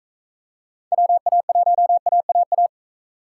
Morse code, heard as a keyed tone, about 38 wpm, 700 Hz. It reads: WA1AAA